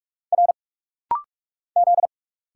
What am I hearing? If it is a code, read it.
Morse code, heard as a keyed tone, about 46 wpm, 690 Hz. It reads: R Z
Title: Single Letters - Rapid-Fire 45wpm